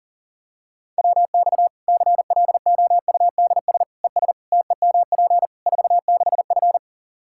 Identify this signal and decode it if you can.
Morse code, heard as a keyed tone, about 40 wpm, 700 Hz. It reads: WX CLOUDS ES TEMP 46F
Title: Practice QSO - Exchange 2 - Info - 40wpm